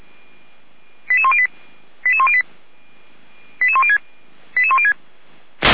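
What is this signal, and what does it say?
Five-tone selective call (ZVEI), heard as quick runs of five tones, tones of about 70 ms. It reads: 88108 88108 88106 88106
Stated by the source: ZVEI five-tone selective call received on an SDR.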